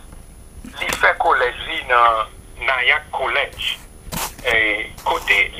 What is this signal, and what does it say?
Emergency Alert System (SAME) tones (header bursts), heard as harsh buzzing bursts, with speech. It is heard from 0.7 to 4.9 s.